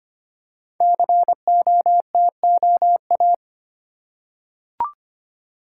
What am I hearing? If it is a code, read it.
Morse code, heard as a keyed tone, about 25 wpm, 700 Hz. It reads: COTOA
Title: Code Groups - 5 Letters - 25wpm